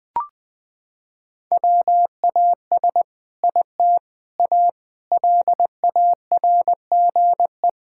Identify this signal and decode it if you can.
Morse code, heard as a keyed tone, about 21 wpm, 700 Hz. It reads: WAS IT A LARGE
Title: Sentences from Top 1000 Words - Review Part 5 - 20wpm